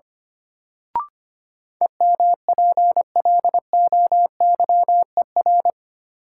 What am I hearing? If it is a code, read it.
Morse code, heard as a keyed tone, about 25 wpm, 700 Hz. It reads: EMPLOYER
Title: Sentences from Top 1600 Words - New Word Drill - 25wpm